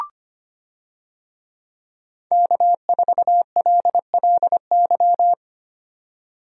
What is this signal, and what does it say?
Morse code, heard as a keyed tone, about 25 wpm, 700 Hz. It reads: K4LLY
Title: USA Call Signs - 5 Characters - 25wpm